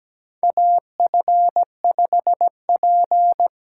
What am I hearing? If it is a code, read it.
Morse code, heard as a keyed tone, about 17 wpm, 700 Hz. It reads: AF5P
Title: K1USN SST QSO Practice - 17wpm